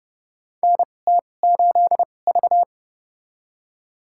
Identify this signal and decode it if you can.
Morse code, heard as a keyed tone, about 30 wpm, spaced as for 22 wpm, 690 Hz. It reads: NT8V
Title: USA Call Signs - 4 Characters - 2x Character Spacing - 30wpm